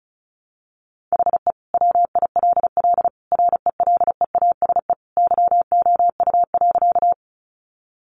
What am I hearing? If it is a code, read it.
Morse code, heard as a keyed tone, about 34 wpm, 700 Hz. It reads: HE WILL RELEASE YOU.
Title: Sentences from Top 1100 Words 35wpm